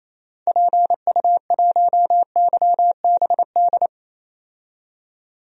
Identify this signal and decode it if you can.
Morse code, heard as a keyed tone, about 28 wpm, 700 Hz. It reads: PU1YBD
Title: DXCC-100 Call Signs - 6 Characters - 28wpm